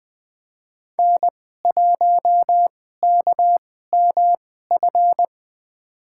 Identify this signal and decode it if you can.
Morse code, heard as a keyed tone, about 20 wpm, spaced as for 14 wpm, 700 Hz. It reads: N1KMF
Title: USA Call Signs - 2x Character Spacing - 20wpm